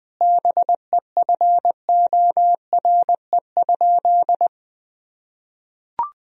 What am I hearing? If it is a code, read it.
Morse code, heard as a keyed tone, about 19 wpm, 700 Hz. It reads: BEFORE?
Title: Sentences from Top 1000 Words - Review Part 3 - 20wpm